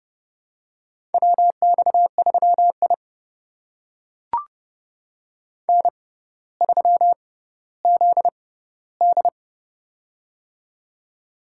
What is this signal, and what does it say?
Morse code, heard as a keyed tone, about 31 wpm, spaced as for 15 wpm, 700 Hz. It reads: WX3I N3ZD
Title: USA Call Signs - 4 Characters - 6x Character Spacing - 30wpm